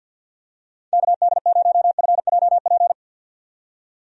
Morse code, heard as a keyed tone, about 48 wpm, 700 Hz. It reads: KD0FJP
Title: USA Call Signs - 6 Characters - 50wpm